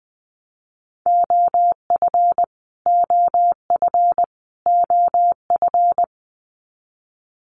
Morse code, heard as a keyed tone, about 19 wpm, 700 Hz. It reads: OF OF OF
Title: Top 500 Words Repeated 20wpm